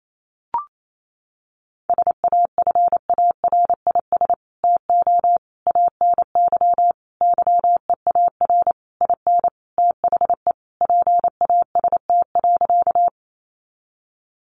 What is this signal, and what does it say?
Morse code, heard as a keyed tone, about 28 wpm, 700 Hz. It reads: SAFARIS TO ANY YEAR IN THE PAST.
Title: A Sound Of Thunder 28wpm